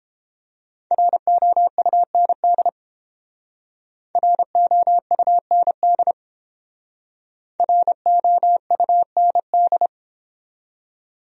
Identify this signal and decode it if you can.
Morse code, heard as a keyed tone, about 29 wpm, 700 Hz. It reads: ROUND ROUND ROUND